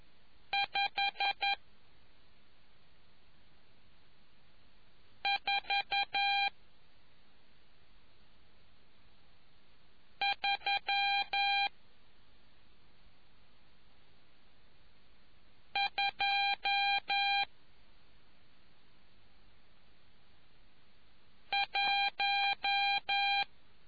Morse code, heard as a keyed tone, about 11 wpm, spaced as for 2 wpm, 800 Hz. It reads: 54321